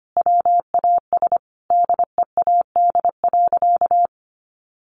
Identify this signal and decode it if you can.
Morse code, heard as a keyed tone, about 25 wpm, 700 Hz. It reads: WAS DEAD.
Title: Sentences from Top 1500 Words 25wpm